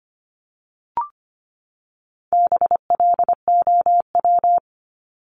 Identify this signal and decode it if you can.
Morse code, heard as a keyed tone, about 25 wpm, 700 Hz. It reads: BLOW
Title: Top 1751-2000 Words 25wpm